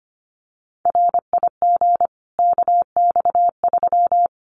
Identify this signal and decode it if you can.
Morse code, heard as a keyed tone, about 25 wpm, 700 Hz. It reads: RIG KX3
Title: Common QSO Phrases 25wpm